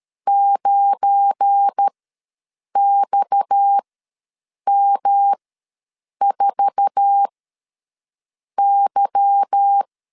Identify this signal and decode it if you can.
Morse code, heard as a keyed tone, about 13 wpm, spaced as for 7 wpm, 800 Hz. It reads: 9XM4Y
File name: LEZIONI / 10 SIGNS / 10B INT ws 12wpm 3527